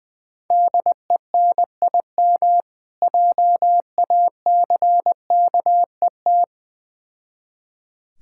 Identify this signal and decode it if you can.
Morse code, heard as a keyed tone, about 20 wpm, 700 Hz. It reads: DENIM JACKET
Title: Difficult Words with Hints - 20wpm